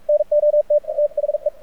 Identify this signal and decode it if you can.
Morse code, heard as a keyed tone, about 45 wpm, 600 Hz, with radio noise. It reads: NOT THE
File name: archived / CRSnet